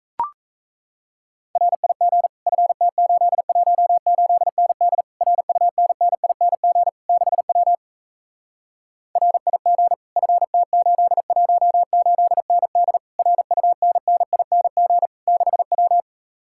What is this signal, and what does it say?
Morse code, heard as a keyed tone, about 37 wpm, 700 Hz. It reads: RIG FT818ND RUNNING 6W RIG FT818ND RUNNING 6W